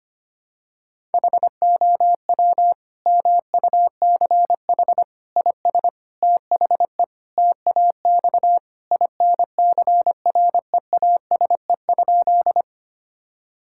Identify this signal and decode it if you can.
Morse code, heard as a keyed tone, about 25 wpm, 700 Hz. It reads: HOW MUCH IS THE TAX INCREASE?
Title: Sentences from Top 1000 Words - Review Part 5 - 25wpm